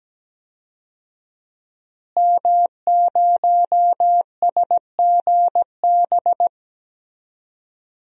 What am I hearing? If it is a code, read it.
Morse code, heard as a keyed tone, about 17 wpm, 700 Hz. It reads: M0SGB
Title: DXCC-100 Call Signs - 5 Characters - Repeated 3x - 17wpm